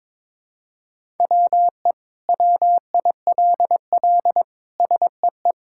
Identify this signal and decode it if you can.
Morse code, heard as a keyed tone, about 21 wpm, 700 Hz. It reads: WE WILL SEE
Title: Sentences from Top 1000 Words - Review Part 5 - 22wpm